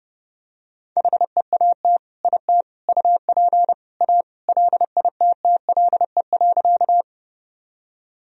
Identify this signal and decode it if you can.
Morse code, heard as a keyed tone, about 30 wpm, 700 Hz. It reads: HEAT IT UP A LITTLE.
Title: Sentences from Top 1000 Words - Review Part 3 - 30wpm